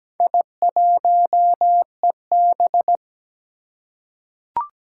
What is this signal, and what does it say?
Morse code, heard as a keyed tone, about 17 wpm, 700 Hz. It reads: I1EB
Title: Four Challenging Letter-Number Combinations 17wpm